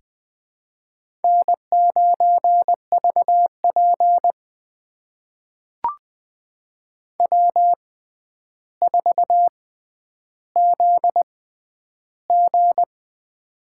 Morse code, heard as a keyed tone, about 21 wpm, spaced as for 10 wpm, 700 Hz. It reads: N9VP W4ZG
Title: USA Call Signs - 4 Characters - 6x Character Spacing - 20wpm